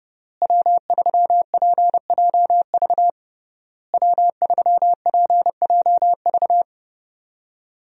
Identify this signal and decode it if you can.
Morse code, heard as a keyed tone, about 32 wpm, 700 Hz. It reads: W3PJV W3PJV
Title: USA Call Signs - 5 Characters - Repeated 3x - 30wpm